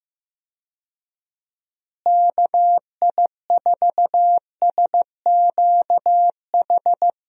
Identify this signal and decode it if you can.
Morse code, heard as a keyed tone, about 15 wpm, 700 Hz. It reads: KI4SQH
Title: USA Call Signs - 6 Characters - Repeated 3x - 15wpm